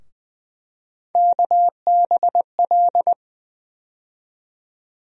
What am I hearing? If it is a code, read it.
Morse code, heard as a keyed tone, about 20 wpm, 700 Hz. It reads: KBL
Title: USA Call Sign Postfixes - 3 Letters 20wpm